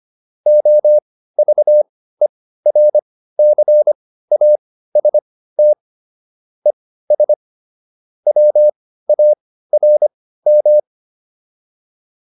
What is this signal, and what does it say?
Morse code, heard as a keyed tone, about 25 wpm, spaced as for 15 wpm, 600 Hz. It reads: OVERCAST ES WARM